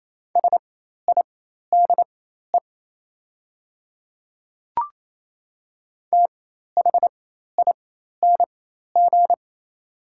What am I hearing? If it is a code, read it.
Morse code, heard as a keyed tone, about 28 wpm, spaced as for 13 wpm, 690 Hz. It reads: SIDE THING
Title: 4x Character Spacing - Top 100 Words - 28wpm